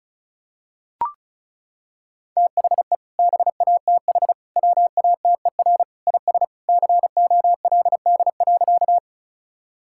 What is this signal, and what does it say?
Morse code, heard as a keyed tone, about 36 wpm, 700 Hz. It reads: THE BATH WATER IS COLD.
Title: Sentences from Top 1600 Words 35wpm